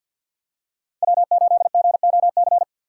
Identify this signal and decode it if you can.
Morse code, heard as a keyed tone, about 50 wpm, 700 Hz. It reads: W8GOC
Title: USA Call Signs - 6x Character Spacing - 50wpm